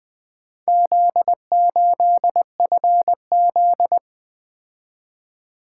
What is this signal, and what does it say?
Morse code, heard as a keyed tone, about 20 wpm, 700 Hz. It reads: Z8FZ
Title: Four Letter-Number Combinations - 20wpm